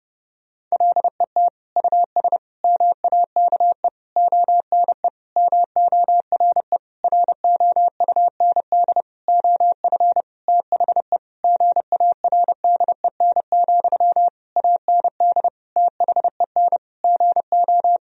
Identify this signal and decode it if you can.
Morse code, heard as a keyed tone, about 32 wpm, 700 Hz. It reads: LET US MAKE ONE MORE ROUND OF THE GARDEN, AND THEN GO
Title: Sentences from Top 1000 Words - Review Part 4 - 30wpm